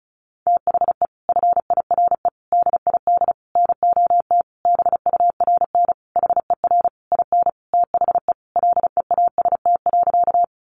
Morse code, heard as a keyed tone, about 34 wpm, 700 Hz. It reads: THE FIRE DID NOT BURN HER IN THE LEAST.
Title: Sentences from Top 1300 Words 35wpm